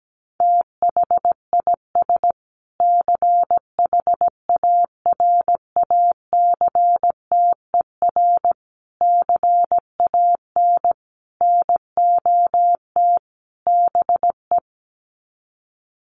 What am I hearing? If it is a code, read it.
Morse code, heard as a keyed tone, about 17 wpm, 700 Hz. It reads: THIS CHARACTER CAN NOT BE
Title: Sets of 5 Words Encore2 17wpm